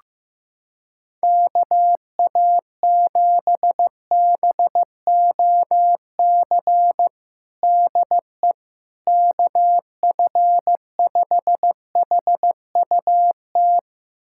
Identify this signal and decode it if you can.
Morse code, heard as a keyed tone, about 15 wpm, 700 Hz. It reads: KA7BOC DE KF5HUT
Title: USA Callsign DE Callsign - 6 Characters - 15wpm